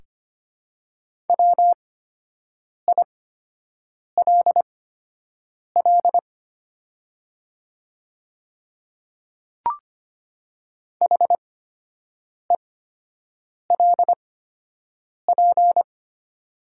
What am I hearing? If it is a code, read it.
Morse code, heard as a keyed tone, about 25 wpm, spaced as for 7 wpm, 700 Hz. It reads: WILL HELP
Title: 8x Character Spacing - Top 200 Words - 25wpm